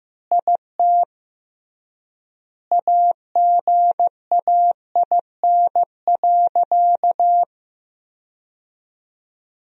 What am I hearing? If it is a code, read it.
Morse code, heard as a keyed tone, about 15 wpm, 700 Hz. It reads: IT AGAIN.
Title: 3x Word-Spacing - Sentences from Top 300 Words - 15wpm